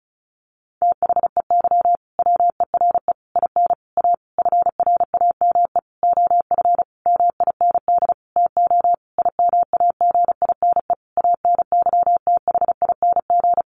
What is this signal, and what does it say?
Morse code, heard as a keyed tone, about 34 wpm, 700 Hz. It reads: THEY WERE IN A FRAME OF MIND TO IMAGINE ANYTHING
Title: Sentences from Top 1400 Words 35wpm